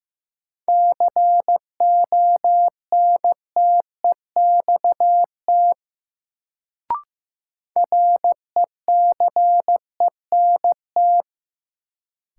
Morse code, heard as a keyed tone, about 15 wpm, 700 Hz. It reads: CONTEXT RECENT
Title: Sentences from Top 1000 Words - Review Part 7 - New Word Drill - 15wpm